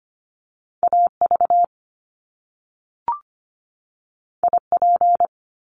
Morse code, heard as a keyed tone, about 25 wpm, 700 Hz. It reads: AV IP
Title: Top Two Letter Combinations in Words 25wpm